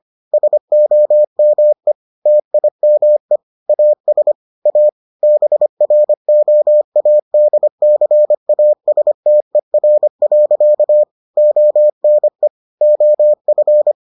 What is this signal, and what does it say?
Morse code, heard as a keyed tone, about 25 wpm, 600 Hz. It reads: SOME TIME AS A BROADCASTER. ONE OF